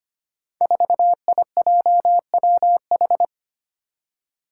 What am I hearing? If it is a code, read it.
Morse code, heard as a keyed tone, about 25 wpm, 700 Hz. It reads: 4IJWH
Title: Five Challenging Letter-Number Combinations 25wpm